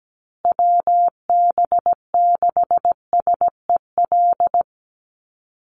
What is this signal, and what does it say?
Morse code, heard as a keyed tone, about 17 wpm, 700 Hz. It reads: WB6SEL
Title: Call Signs Repeated 3x - 17wpm